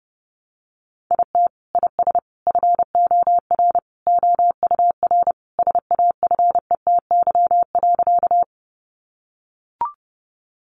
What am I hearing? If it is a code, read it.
Morse code, heard as a keyed tone, about 30 wpm, 700 Hz. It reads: IT IS FOR OUR SAFETY.
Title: Sentences from Top 700 Words 30wpm